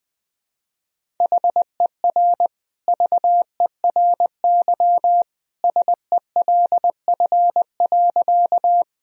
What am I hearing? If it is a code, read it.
Morse code, heard as a keyed tone, about 21 wpm, 700 Hz. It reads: HER VERY SELF.
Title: Sentences from Top 1000 Words - Review Part 1 - 20wpm